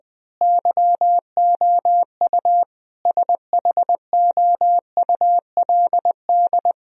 Morse code, heard as a keyed tone, about 20 wpm, 700 Hz. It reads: YOU SHOULD
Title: Sentences from Top 1000 Words - Review Part 1 - 20wpm